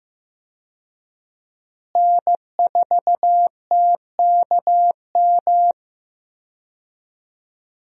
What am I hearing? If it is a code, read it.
Morse code, heard as a keyed tone, about 15 wpm, 700 Hz. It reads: N4TKM
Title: USA Call Signs - 5 Characters - Repeated 3x - 15wpm